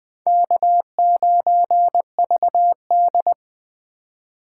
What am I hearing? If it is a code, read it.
Morse code, heard as a keyed tone, about 20 wpm, 700 Hz. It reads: K9VD
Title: 2023 SOTA Chaser Call Signs - 20wpm